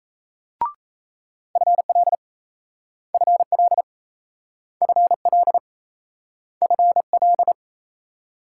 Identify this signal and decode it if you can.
Morse code, heard as a keyed tone, about 38 wpm, 700 Hz. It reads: FL FL FL FL